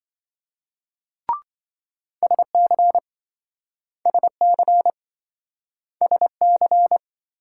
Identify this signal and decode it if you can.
Morse code, heard as a keyed tone, about 27 wpm, 700 Hz. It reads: SC SC SC